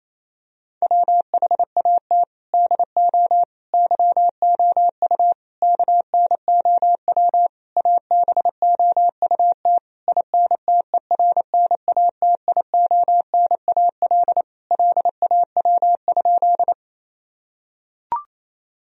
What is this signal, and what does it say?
Morse code, heard as a keyed tone, about 27 wpm, 700 Hz. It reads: WHAT DO YOU KNOW ABOUT INTERNATIONAL LAW?
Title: Sentences from Top 1000 Words - Review Part 4 - 28wpm